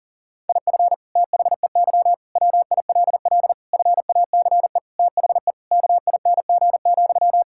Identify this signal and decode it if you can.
Morse code, heard as a keyed tone, about 37 wpm, 700 Hz. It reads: IF THEY WILL FACE THE KING,
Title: Sentences from Top 1000 Words - Review Part 6 - 40wpm